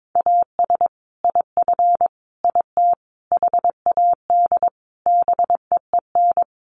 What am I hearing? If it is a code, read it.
Morse code, heard as a keyed tone, about 22 wpm, 700 Hz. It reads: AS IF IT HAD BEEN
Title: Sets of 5 Words Encore2 22wpm